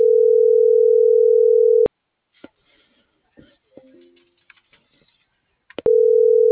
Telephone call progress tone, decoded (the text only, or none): ringback tone